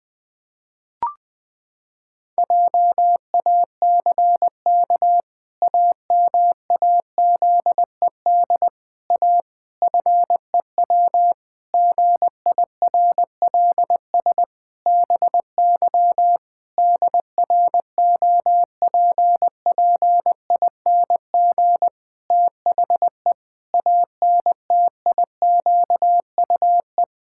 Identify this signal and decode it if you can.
Morse code, heard as a keyed tone, about 20 wpm, 700 Hz. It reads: JACK AMAZED A FEW GIRLS BY DROPPING THE ANTIQUE
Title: Pangram Challenge v2 - 20wpm